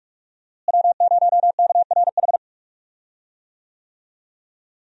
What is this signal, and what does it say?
Morse code, heard as a keyed tone, about 45 wpm, 700 Hz. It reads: W0KRH
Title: USA Call Signs - 5 Characters - Repeated 2x - 45wpm